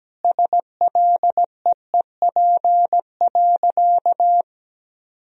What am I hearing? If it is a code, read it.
Morse code, heard as a keyed tone, about 17 wpm, 700 Hz. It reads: SLEEP.
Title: Sentences from Top 1000 Words - Review Part 2 - 17wpm